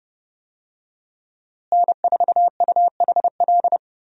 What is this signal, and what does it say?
Morse code, heard as a keyed tone, about 29 wpm, 700 Hz. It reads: N4UHL
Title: USA Call Signs - 5 Characters - Repeated 3x - 30wpm